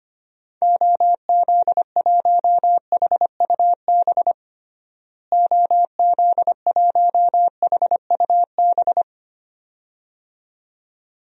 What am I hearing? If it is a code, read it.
Morse code, heard as a keyed tone, about 25 wpm, 700 Hz. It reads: OZ1HUB OZ1HUB
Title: DXCC-100 Call Signs - 6 Characters - Repeated 2x - 25wpm